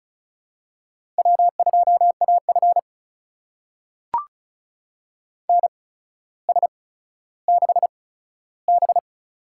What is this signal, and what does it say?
Morse code, heard as a keyed tone, about 36 wpm, spaced as for 15 wpm, 700 Hz. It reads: W2AF NS6B
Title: USA Call Signs - 4 Characters - 8x Character Spacing - 35wpm